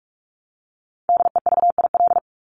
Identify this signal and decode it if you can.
Morse code, heard as a keyed tone, about 44 wpm, 700 Hz. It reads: DEVIL
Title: Top 1751-2000 Words 45wpm